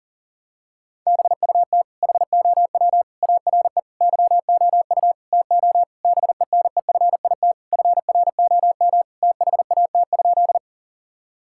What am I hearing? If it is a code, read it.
Morse code, heard as a keyed tone, about 40 wpm, 700 Hz. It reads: BUT HOW ARE YOU TO BENEFIT FROM THAT?
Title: Sentences from Top 1000 Words - Review Part 6 - 40wpm